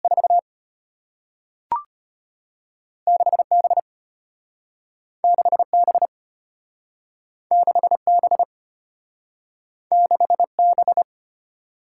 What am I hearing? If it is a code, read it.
Morse code, heard as a keyed tone, about 34 wpm, 700 Hz. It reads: 4 6B 6B 6B 6B